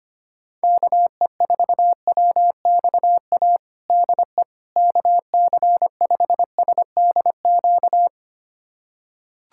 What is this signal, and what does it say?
Morse code, heard as a keyed tone, about 25 wpm, 700 Hz. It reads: KE4WXA DE KC5SDQ
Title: USA Callsign DE Callsign - 6 Characters - 25wpm